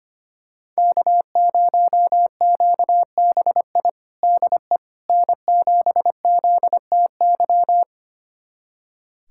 Morse code, heard as a keyed tone, about 25 wpm, 700 Hz. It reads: K0QBI DE N7ZTY
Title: USA Callsign DE Callsign - 5 Characters - 25wpm